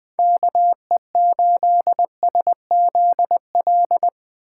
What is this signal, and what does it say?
Morse code, heard as a keyed tone, about 21 wpm, 700 Hz. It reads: KE8SZL
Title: 2023 POTA Activator Call Signs - 20wpm